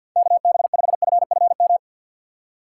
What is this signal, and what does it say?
Morse code, heard as a keyed tone, about 48 wpm, 700 Hz. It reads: KB5FFM
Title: USA Call Signs - 6 Characters - 6x Character Spacing - 50wpm